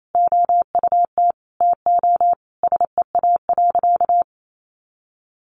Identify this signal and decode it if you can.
Morse code, heard as a keyed tone, about 28 wpm, 700 Hz. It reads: OUT TO SEA.
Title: Sentences from Top 400 Words 28wpm